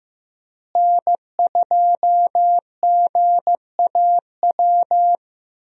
Morse code, heard as a keyed tone, about 14 wpm, 700 Hz. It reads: N2GAW
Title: USA Call Signs - 5 Characters - Repeated 3x - 15wpm